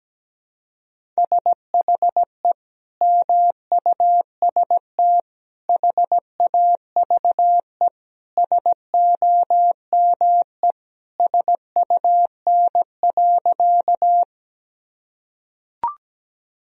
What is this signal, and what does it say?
Morse code, heard as a keyed tone, about 17 wpm, 700 Hz. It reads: SHE MUST HAVE SOME SUN.
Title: Sentences from Top 1000 Words - Review Part 1 - 17wpm